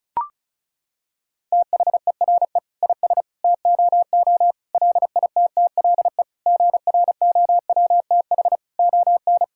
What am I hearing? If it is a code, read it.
Morse code, heard as a keyed tone, about 36 wpm, 700 Hz. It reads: THERE IS TOO LITTLE GROWTH ON
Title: Sentences from Top 1000 Words - Review Part 5 - 35wpm